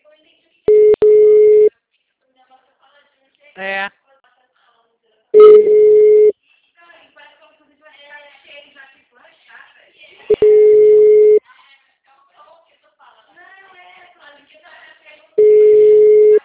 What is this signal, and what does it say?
Ringback tone (Europe, 425 Hz), with speech.
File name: out- -2034- - - .37273